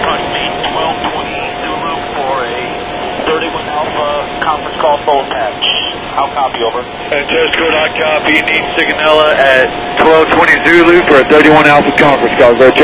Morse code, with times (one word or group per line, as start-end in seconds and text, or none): none